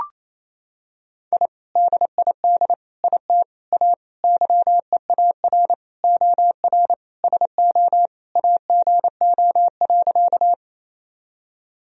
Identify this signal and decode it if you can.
Morse code, heard as a keyed tone, about 27 wpm, 700 Hz. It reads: I DID IT A YEAR OR SO AGO.
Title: Sentences from Top 1000 Words - Review Part 3 - 28wpm